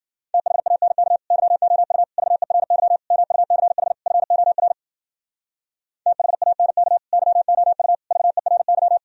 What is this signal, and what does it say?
Morse code, heard as a keyed tone, about 59 wpm, 700 Hz. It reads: THANK YOU VERY MUCH FOR THANK YOU VERY